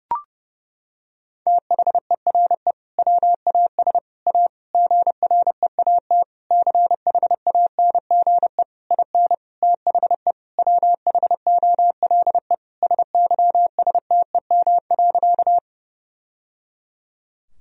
Morse code, heard as a keyed tone, about 30 wpm, 700 Hz. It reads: THERE WAS A GREAT CHANGE IN THE WHOLE SYSTEM.
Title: Sentences from Top 1000 Words - Review Part 3 - 30wpm